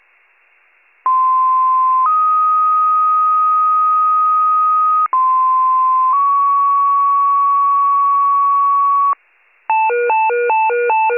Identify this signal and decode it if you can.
Two-tone page, heard as a one-second tone, then a three-second tone, about 1040 then 1250 Hz, about 1040 then 1120 Hz. It is heard from 1.1 to 9.1 s.